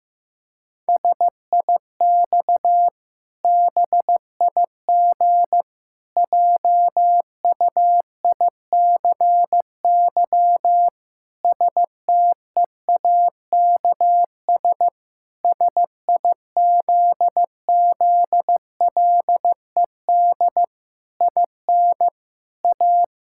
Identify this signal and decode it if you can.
Morse code, heard as a keyed tone, about 15 wpm, 700 Hz. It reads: SIX BIG JUICY STEAKS SIZZLED IN A
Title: Pangram Challenge v2 - 15wpm